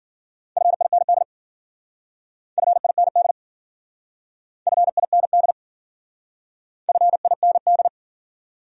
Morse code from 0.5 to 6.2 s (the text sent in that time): FIND FIND FIND